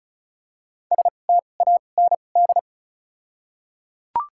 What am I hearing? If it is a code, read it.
Morse code, heard as a keyed tone, about 34 wpm, spaced as for 25 wpm, 700 Hz. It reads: STAND
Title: 2x Character Spacing - Top 300 Words - 35wpm